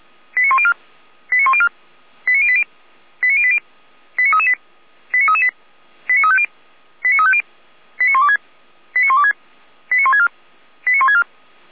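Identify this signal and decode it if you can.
Five-tone selective call (ZVEI), heard as quick runs of five tones, tones of about 70 ms. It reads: 89104 89104 89080 89080 89338 89338 89350 89350 89126 89126 89164 89164
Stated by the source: ZVEI five-tone call received by SDR.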